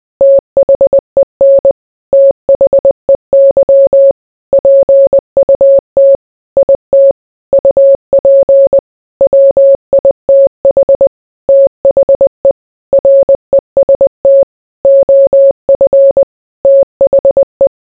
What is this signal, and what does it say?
Morse code, heard as a keyed tone, about 20 wpm, 550 Hz. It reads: THEN THEY PUT IT UP WITH THE REST OF THE